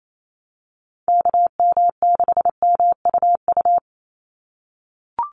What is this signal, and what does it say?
Morse code, heard as a keyed tone, about 28 wpm, 700 Hz. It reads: KM6MUU
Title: USA Call Signs 28wpm